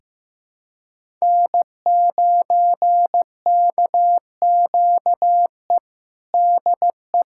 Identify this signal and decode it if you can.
Morse code, heard as a keyed tone, about 16 wpm, 700 Hz. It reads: N9KQE DE
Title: USA Callsign DE Callsign - 5 Characters - 15wpm